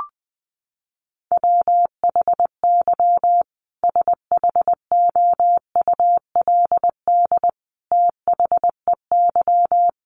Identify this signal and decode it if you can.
Morse code, heard as a keyed tone, about 19 wpm, 700 Hz. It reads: WHY SHOULD THEY
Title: Sentences from Top 1100 Words 20wpm